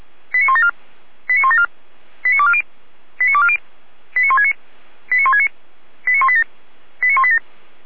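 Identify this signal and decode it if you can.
Five-tone selective call (ZVEI), heard as quick runs of five tones, tones of about 70 ms. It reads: 89174 89174 89240 89240 89169 89169 89187 89187
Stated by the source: ZVEI five-tone call received by SDR.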